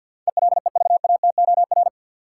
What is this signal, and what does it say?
Morse code, heard as a keyed tone, about 52 wpm, 700 Hz. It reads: ELEVATOR
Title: Sentences from Top 1600 Words - New Word Drill Alt - 50wpm